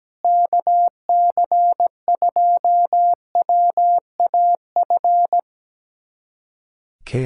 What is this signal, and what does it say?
Morse code, heard as a keyed tone, about 17 wpm, 700 Hz. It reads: KC2WAF
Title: USA Callsign DE Callsign 17wpm